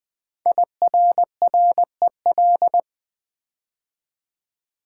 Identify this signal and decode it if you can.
Morse code, heard as a keyed tone, about 20 wpm, 700 Hz. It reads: IRREL
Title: Build A Word - 5 Letters At A Time - 10-Letter Words - 20wpm